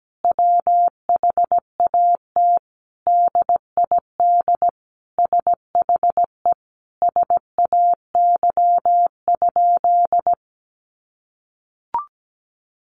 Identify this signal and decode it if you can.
Morse code, heard as a keyed tone, about 17 wpm, 700 Hz. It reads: WHAT DID SHE SAY?
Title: Sentences from Top 500 Words 17wpm